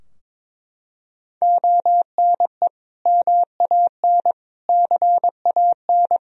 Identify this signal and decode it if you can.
Morse code, heard as a keyed tone, about 22 wpm, 700 Hz. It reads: ONE MAN CAN
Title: Sentences from Top 1000 Words - Review Part 4 - 22wpm